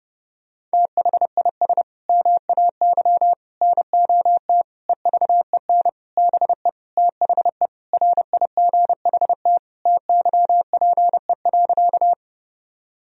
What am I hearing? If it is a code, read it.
Morse code, heard as a keyed tone, about 30 wpm, 700 Hz. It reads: THIS MAY NOT EVEN BE THE RIGHT TYPE.
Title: Sentences from Top 1000 Words - Review Part 3 - 30wpm